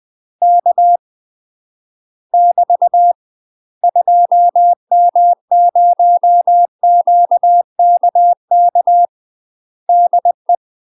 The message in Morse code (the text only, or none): K <BT> 2M0QKK DE